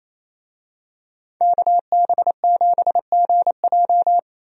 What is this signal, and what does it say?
Morse code, heard as a keyed tone, about 27 wpm, 700 Hz. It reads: KB7GJ
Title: USA Call Signs - 5 Characters - Repeated 3x - 28wpm